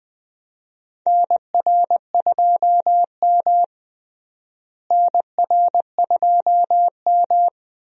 Morse code, heard as a keyed tone, about 21 wpm, 700 Hz. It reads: NR2M NR2M
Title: USA Call Signs - 4 Characters - Repeated 3x - 20wpm